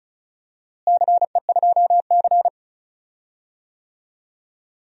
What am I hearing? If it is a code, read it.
Morse code, heard as a keyed tone, about 36 wpm, 700 Hz. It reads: CE2C
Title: DXCC-100 Call Signs - 4 Characters - Repeated 2x - 35wpm